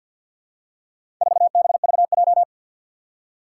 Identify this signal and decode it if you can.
Morse code, heard as a keyed tone, about 49 wpm, 700 Hz. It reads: 4BVJ
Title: Four Challenging Letter-Number Combinations 50wpm